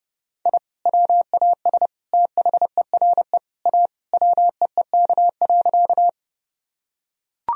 Morse code, heard as a keyed tone, about 30 wpm, 700 Hz. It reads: I WAS THERE A WEEK.
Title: Sentences from Top 1000 Words - Review Part 2 - 30wpm